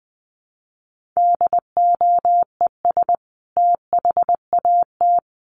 Morse code, heard as a keyed tone, about 20 wpm, 700 Hz. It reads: DOES THAT
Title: Sentences from Top 1300 Words 20wpm